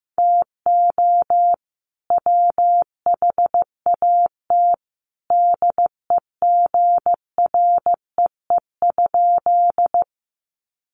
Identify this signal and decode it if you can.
Morse code, heard as a keyed tone, about 15 wpm, 700 Hz. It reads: TO WHAT DEGREE?
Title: Sentences from Top 1500 Words 15wpm